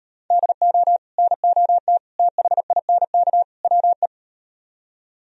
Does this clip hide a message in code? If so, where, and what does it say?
Yes: Morse code from 0.3 to 4.1 s, heard as a keyed tone, about 38 wpm, 700 Hz. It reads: DO NOT THINK WE